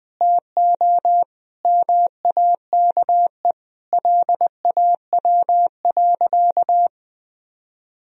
Morse code, heard as a keyed tone, about 20 wpm, 700 Hz. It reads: TO MAKE LAW.
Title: Sentences from Top 1000 Words - Review Part 3 - 20wpm